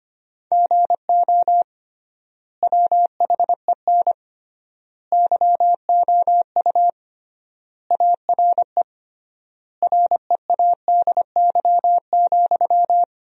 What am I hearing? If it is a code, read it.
Morse code, heard as a keyed tone, about 25 wpm, 700 Hz. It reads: GO WHEN YOU ARE READY,
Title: 3x Word-Spacing - Sentences from Top 500 Words - 25wpm